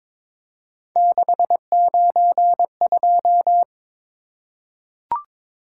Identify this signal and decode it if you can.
Morse code, heard as a keyed tone, about 23 wpm, 700 Hz. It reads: 692
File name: practice / 22wpm / Three Letter-Number Combinations - 22wpm